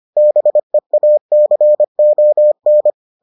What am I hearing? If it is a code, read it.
Morse code, heard as a keyed tone, about 25 wpm, 600 Hz. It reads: BEACON